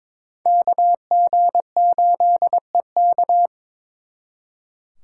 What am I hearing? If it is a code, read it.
Morse code, heard as a keyed tone, about 26 wpm, 700 Hz. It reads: KG8EK
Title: Speed Racing - USA 2x2 Call Signs 22wpm